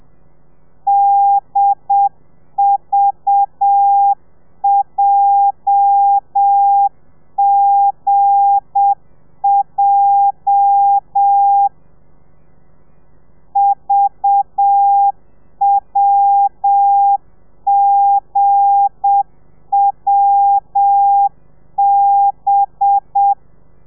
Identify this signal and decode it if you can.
Morse code, heard as a keyed tone, about 7 wpm, 800 Hz. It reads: DVJGJ VWGWB
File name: telegrafia / i2tfj / Lez8 35